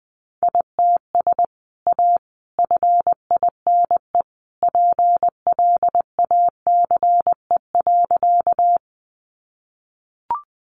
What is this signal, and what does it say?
Morse code, heard as a keyed tone, about 20 wpm, 700 Hz. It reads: ITS A FINE PLACE.
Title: Sentences from Top 600 Words 20wpm